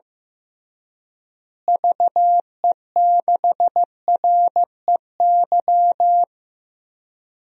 Morse code, heard as a keyed tone, about 15 wpm, 700 Hz. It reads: VE6REY
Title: DXCC-100 Call Signs - 6 Characters - Repeated 3x - 15wpm